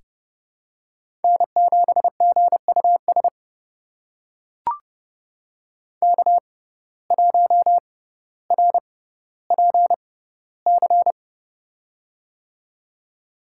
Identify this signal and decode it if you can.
Morse code, heard as a keyed tone, about 30 wpm, spaced as for 15 wpm, 700 Hz. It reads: N7GUS K1RPC
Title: USA Call Signs - 5 Characters - 6x Character Spacing - 30wpm